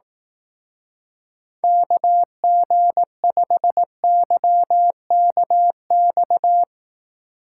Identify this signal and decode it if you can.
Morse code, heard as a keyed tone, about 18 wpm, 700 Hz. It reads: KG5YKX
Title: Speed Racing - USA 2x3 Call Signs 15wpm